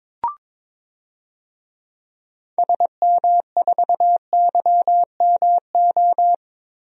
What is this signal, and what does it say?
Morse code, heard as a keyed tone, about 22 wpm, 700 Hz. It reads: SM4YMO
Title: DXCC-100 Call Signs - 6 Characters - 22wpm